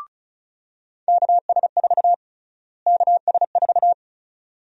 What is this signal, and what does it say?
Morse code, heard as a keyed tone, about 36 wpm, 700 Hz. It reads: KS4 KS4
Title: USA Call Sign Prefixes - Repeated 2x - 35wpm